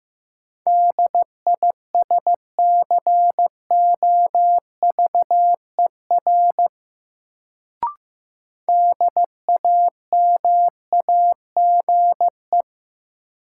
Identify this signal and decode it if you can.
Morse code, heard as a keyed tone, about 15 wpm, 700 Hz. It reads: DISCOVER DAMAGE
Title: Sentences from Top 1100 Words - New Word Drill - 15wpm